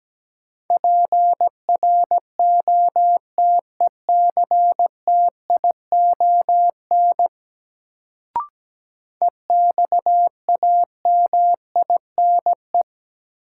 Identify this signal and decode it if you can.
Morse code, heard as a keyed tone, about 17 wpm, 700 Hz. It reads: PROTECTION EXAMINE
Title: Sentences from Top 1600 Words - New Word Drill - 17wpm